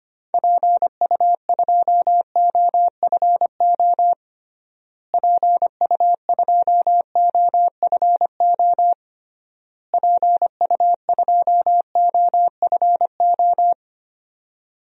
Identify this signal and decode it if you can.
Morse code, heard as a keyed tone, about 25 wpm, 700 Hz. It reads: PU2OFO PU2OFO PU2OFO
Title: DXCC-100 Call Signs - 6 Characters - Repeated 3x - 25wpm